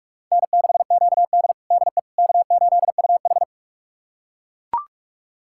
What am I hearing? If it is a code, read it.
Morse code, heard as a keyed tone, about 41 wpm, 700 Hz. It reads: N6QD DE K8UH
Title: USA Callsign DE Callsign - 4 Characters - 45wpm